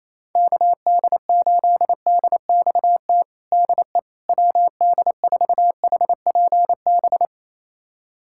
Morse code, heard as a keyed tone, about 28 wpm, 700 Hz. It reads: KD8DXT DE WD4HPB
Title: USA Callsign DE Callsign - 6 Characters - 28wpm